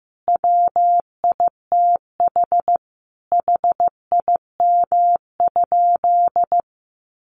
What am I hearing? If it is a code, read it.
Morse code, heard as a keyed tone, about 15 wpm, 700 Hz. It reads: WITH HIM?
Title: Sentences from Top 600 Words 15wpm